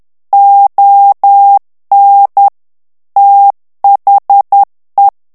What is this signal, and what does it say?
Morse code, heard as a keyed tone, about 10 wpm, 800 Hz. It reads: ON THE
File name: morse mp3 / ANGLAIS texte clair 2 vitesse 10 mots minute